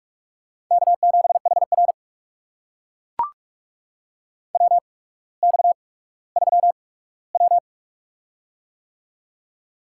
Morse code, heard as a keyed tone, about 42 wpm, spaced as for 19 wpm, 700 Hz. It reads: K7HR WX3W
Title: USA Call Signs - 4 Characters - 8x Character Spacing - 45wpm